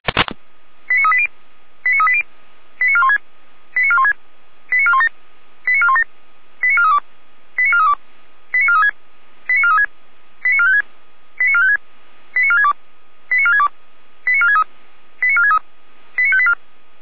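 Five-tone selective call (ZVEI), heard as quick runs of five tones, tones of about 70 ms. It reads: 89380 89380 89416 89416 89417 89417 89432 89432 89436 89436 89456 89456 89462 89462 89463 89463 89584
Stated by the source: ZVEI five-tone selective call received by SDR.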